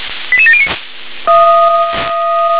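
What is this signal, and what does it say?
Five-tone selective call (ZVEI), heard as quick runs of five tones, tones of about 70 ms. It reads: 88709
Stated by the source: ZVEI five-tone selective call received on SDR.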